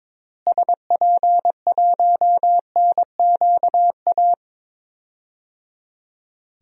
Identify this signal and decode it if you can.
Morse code, heard as a keyed tone, about 21 wpm, 700 Hz. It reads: SP1NQA
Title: DXCC-100 Call Signs - 6 Characters - Repeated 3x - 22wpm